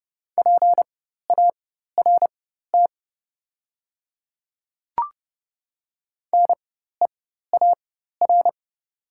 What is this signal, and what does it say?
Morse code, heard as a keyed tone, about 28 wpm, spaced as for 14 wpm, 700 Hz. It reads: PART NEAR
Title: 4x Character Spacing - Top 300 Words - 30wpm